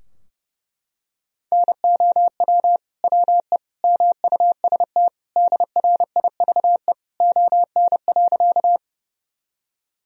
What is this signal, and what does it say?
Morse code, heard as a keyed tone, about 32 wpm, 700 Hz. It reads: NOW WE MUST DRIVE ON.
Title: Sentences from Top 1000 Words - Review Part 2 - 30wpm